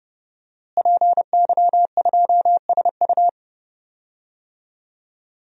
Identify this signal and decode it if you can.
Morse code, heard as a keyed tone, about 30 wpm, 700 Hz. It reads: PY2SU
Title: DXCC-100 Call Signs - 5 Characters - Repeated 2x - 30wpm